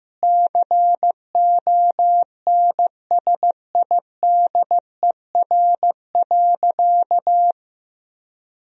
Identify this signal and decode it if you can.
Morse code, heard as a keyed tone, about 16 wpm, 700 Hz. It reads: CONSIDER.
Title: Sentences from Top 1000 Words - Review Part 4 - 15wpm